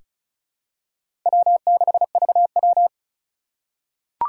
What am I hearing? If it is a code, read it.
Morse code, heard as a keyed tone, about 36 wpm, 700 Hz. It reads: W6VW
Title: USA Call Signs - 4 Characters - 2x Character Spacing - 35wpm